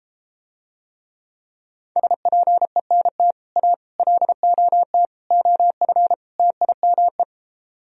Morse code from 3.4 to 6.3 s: A LOT OF

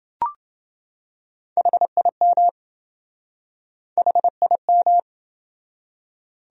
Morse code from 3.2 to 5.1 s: HIM

